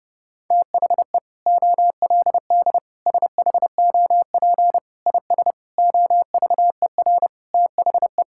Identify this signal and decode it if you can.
Morse code, heard as a keyed tone, about 30 wpm, 700 Hz. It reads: THE OLD SHOP IS OVER THE
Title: Sentences from Top 1600 Words 30wpm